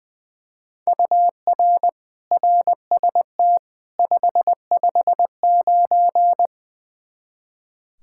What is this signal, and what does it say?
Morse code, heard as a keyed tone, about 20 wpm, 700 Hz. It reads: UR RST 559
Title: Practice QSO - Exchange 1 - Beginning - 20wpm